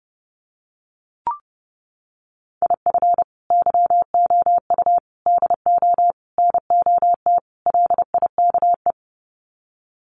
Morse code, heard as a keyed tone, about 28 wpm, 700 Hz. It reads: IF YOU DO NOT LIKE
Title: Sets of 5 Words 30wpm